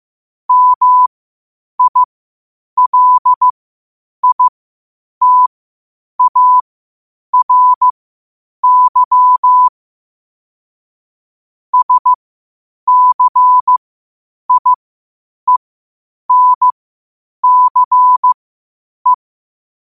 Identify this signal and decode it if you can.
Morse code, heard as a keyed tone, about 15 wpm, spaced as for 8 wpm, 1000 Hz. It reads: MILITARY SCIENCE